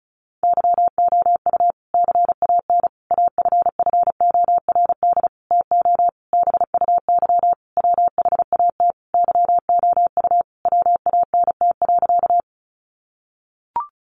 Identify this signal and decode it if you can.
Morse code, heard as a keyed tone, about 34 wpm, 700 Hz. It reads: YOU CAN AFFORD TO BUY WHAT YOU WANT.
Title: Sentences from Top 1100 Words 35wpm